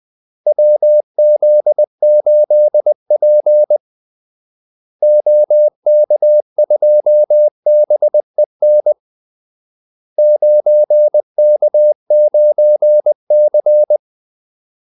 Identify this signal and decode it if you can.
Morse code, heard as a keyed tone, about 20 wpm, 600 Hz. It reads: WZ8P OK2BEN 9K9C
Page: Kaldesignaler 20 wpm